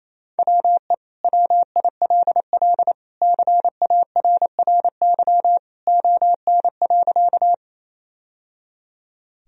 Morse code, heard as a keyed tone, about 28 wpm, 700 Hz. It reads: WE WILL CARRY ON.
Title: Sentences from Top 1000 Words - Review Part 1 - 28wpm